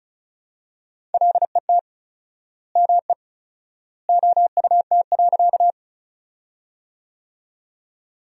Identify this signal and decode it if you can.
Morse code, heard as a keyed tone, about 36 wpm, 700 Hz. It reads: LET ME OUT.
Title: 4x Word-Spacing - Sentences from Top 300 Words - 35wpm